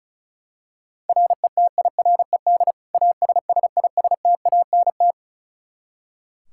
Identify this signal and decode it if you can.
Morse code, heard as a keyed tone, about 36 wpm, 700 Hz. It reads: RETIRED ASSISTANT
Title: Practice QSO - Exchange 3 - More Info - 35wpm